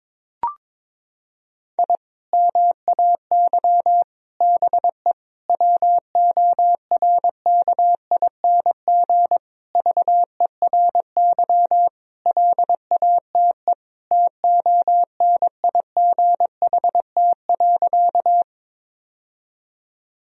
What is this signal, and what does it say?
Morse code, heard as a keyed tone, about 22 wpm, 700 Hz. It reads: I MAY BE WORKING VERY LATE TONIGHT.
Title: Sentences from Top 1000 Words - Review Part 4 - 22wpm